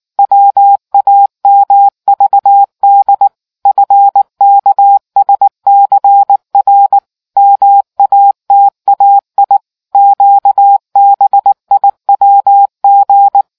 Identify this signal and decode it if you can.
Morse code, heard as a keyed tone, about 19 wpm, 800 Hz. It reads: WAMVD FKSCR MATAI QBIWG